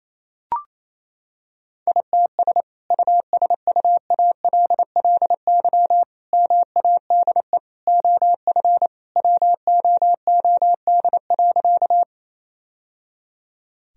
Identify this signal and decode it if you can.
Morse code, heard as a keyed tone, about 27 wpm, 700 Hz. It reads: ITS USUALLY MADE OF WOOD.
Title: Sentences from Top 1000 Words - Review Part 3 - 28wpm